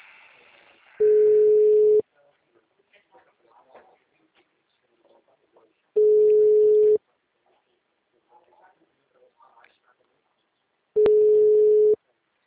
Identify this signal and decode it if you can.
Ringback tone (Europe, 425 Hz), with speech.